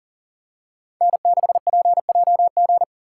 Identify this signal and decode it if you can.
Morse code, heard as a keyed tone, about 40 wpm, 700 Hz. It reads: N6PJG
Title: USA Call Signs - 5 Characters - 3x Character Spacing - 40wpm